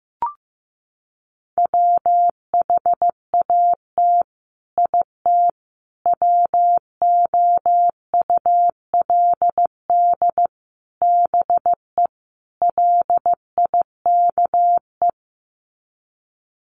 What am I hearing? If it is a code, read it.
Morse code, heard as a keyed tone, about 15 wpm, 700 Hz. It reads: WHAT IT WOULD BE LIKE
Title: Sets of 5 Words Encore 15wpm